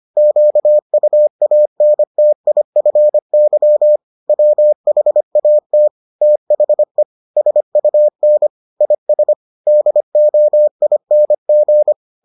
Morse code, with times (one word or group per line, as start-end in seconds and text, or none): 0.2-4.0 QUANTIFY
4.3-5.9 WHAT
6.2-7.0 THE
7.4-8.5 SUN
8.8-9.3 IS
9.7-11.9 DOING